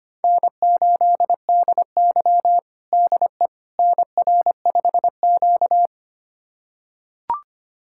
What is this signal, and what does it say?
Morse code, heard as a keyed tone, about 25 wpm, 700 Hz. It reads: N8DY DE NR5Q
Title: USA Callsign DE Callsign - 4 Characters - 25wpm